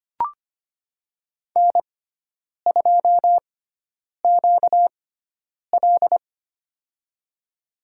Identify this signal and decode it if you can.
Morse code, heard as a keyed tone, about 25 wpm, spaced as for 9 wpm, 700 Hz. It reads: N2QL
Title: USA Call Signs - 4 Characters - 6x Character Spacing - 25wpm